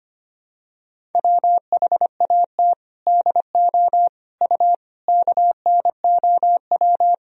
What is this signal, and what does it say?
Morse code, heard as a keyed tone, about 25 wpm, 700 Hz. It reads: WHAT DO U KNOW
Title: Sentences from Top 100 Words and Abbreviations 25wpm